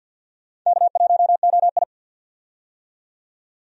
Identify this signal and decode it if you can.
Morse code, heard as a keyed tone, about 52 wpm, 700 Hz. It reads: K1OI
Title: USA Call Signs - 4 Characters - 50wpm